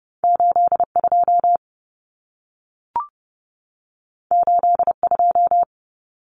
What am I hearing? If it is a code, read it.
Morse code, heard as a keyed tone, about 30 wpm, 700 Hz. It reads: 82 82
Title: Common Cut Numbers - 2-Digit 30wpm